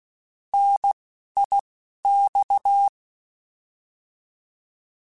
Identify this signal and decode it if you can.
Morse code, heard as a keyed tone, about 16 wpm, 800 Hz. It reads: N I X
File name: Contrasts / Contrasts-mp3s / 3LtrContrasts @10